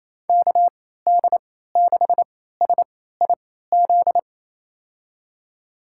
Morse code, heard as a keyed tone, about 28 wpm, spaced as for 16 wpm, 700 Hz. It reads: KD6SIZ
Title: USA Call Signs - 6 Characters - 3x Character Spacing - 28wpm